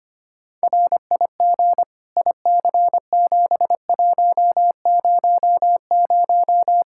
Morse code, heard as a keyed tone, about 25 wpm, 700 Hz. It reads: RIG IC7100
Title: K1USN SST QSO Practice 25wpm